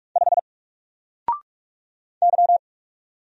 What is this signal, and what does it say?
Morse code, heard as a keyed tone, about 45 wpm, 700 Hz. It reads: 5 Y
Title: Single Character (Letter:Number:Punctuation:Prosign) 45wpm